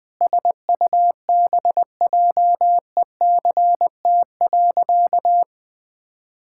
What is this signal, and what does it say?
Morse code, heard as a keyed tone, about 20 wpm, 700 Hz. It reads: SUBJECT.
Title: Sentences from Top 1000 Words - Review Part 6 - 20wpm